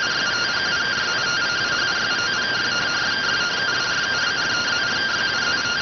Two-tone teleprinter signal, about 170 Hz shift.